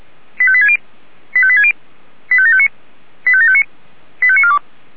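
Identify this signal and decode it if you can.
Five-tone selective call (ZVEI), heard as quick runs of five tones, tones of about 70 ms. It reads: 85860 85860 85759 85759 85942
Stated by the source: ZVEI five-tone selective call 85729 received on SDR.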